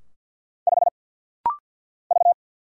Morse code, heard as a keyed tone, about 50 wpm, 700 Hz. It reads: 5 V
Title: Single Letter-Number Alt - Rapid-Fire 50wpm